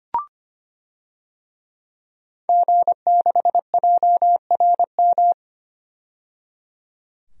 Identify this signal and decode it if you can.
Morse code, heard as a keyed tone, about 25 wpm, 700 Hz. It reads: G6JRM
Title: DXCC-100 Call Signs - 5 Characters - 25wpm